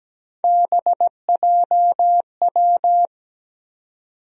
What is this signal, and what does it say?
Morse code, heard as a keyed tone, about 17 wpm, 700 Hz. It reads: BJW
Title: USA Call Sign Postfixes - 3 Letters 17wpm